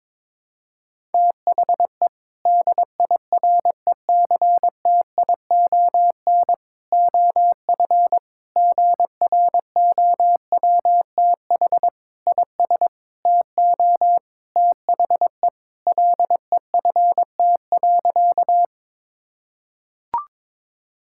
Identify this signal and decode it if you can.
Morse code, heard as a keyed tone, about 22 wpm, 700 Hz. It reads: THE DIRECTION OF GROWTH IS TO THE LEFT.
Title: Sentences from Top 1000 Words - Review Part 5 - 22wpm